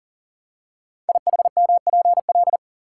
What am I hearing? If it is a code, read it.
Morse code, heard as a keyed tone, about 41 wpm, 700 Hz. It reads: IHMPL
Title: Code Groups - 5 Letters - 40wpm